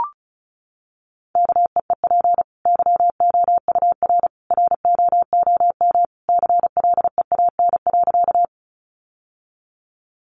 Morse code, heard as a keyed tone, about 34 wpm, 700 Hz. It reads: KEEP YOUR ROOM CLEAN.
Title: Sentences from Top 1100 Words 35wpm